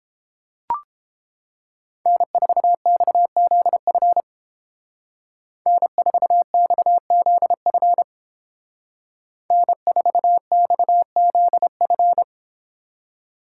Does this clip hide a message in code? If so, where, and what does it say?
Yes: Morse code from 2.1 to 12.2 s, heard as a keyed tone, about 29 wpm, 700 Hz. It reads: N4XZF N4XZF N4XZF